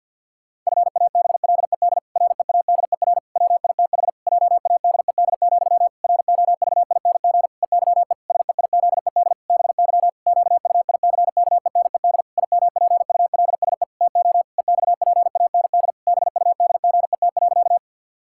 Morse code, heard as a keyed tone, about 50 wpm, 700 Hz. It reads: FABLED READER WITH JADED, ROVING EYE SEIZED BY QUICKENED IMPULSE TO EXPAND BUDGET.
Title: Pangram Challenge v2 - 50wpm